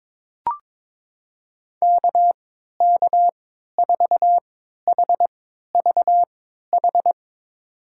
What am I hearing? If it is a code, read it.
Morse code, heard as a keyed tone, about 21 wpm, spaced as for 12 wpm, 700 Hz. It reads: KK4HVH
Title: USA Call Signs - 6 Characters - 3x Character Spacing - 22wpm